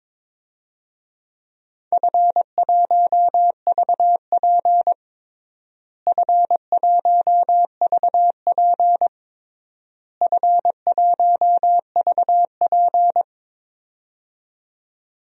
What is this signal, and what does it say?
Morse code, heard as a keyed tone, about 21 wpm, 700 Hz. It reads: F1VP F1VP F1VP
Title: DXCC-100 Call Signs - 4 Characters - Repeated 3x - 22wpm